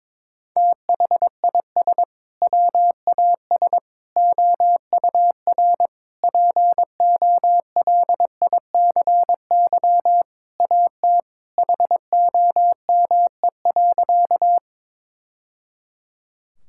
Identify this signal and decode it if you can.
Morse code, heard as a keyed tone, about 22 wpm, 700 Hz. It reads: THIS WAS OUR POLICY AT HOME.
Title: Sentences from Top 1000 Words - Review Part 5 - 22wpm